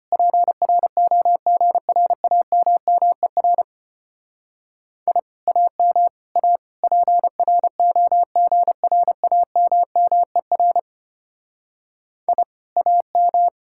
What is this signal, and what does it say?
Morse code, heard as a keyed tone, about 29 wpm, 700 Hz. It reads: PROGRAMMER I AM A PROGRAMMER I AM
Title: Speed-Racing Common QSO Phrases Encore - 25wpm